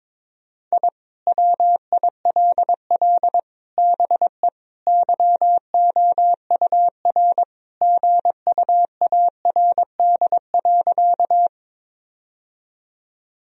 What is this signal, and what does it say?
Morse code, heard as a keyed tone, about 22 wpm, 700 Hz. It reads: I WILL BE YOUR GUARD.
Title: Sentences from Top 1000 Words - Review Part 4 - 22wpm